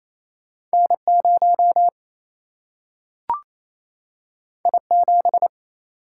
Morse code, heard as a keyed tone, about 28 wpm, 700 Hz. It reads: N 0 I 7
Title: Single-Letter Single-Number 28wpm